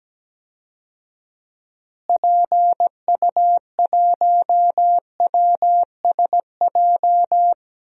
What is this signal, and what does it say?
Morse code, heard as a keyed tone, about 17 wpm, 700 Hz. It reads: PU1WSJ
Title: DXCC-100 Call Signs - 6 Characters - Repeated 3x - 17wpm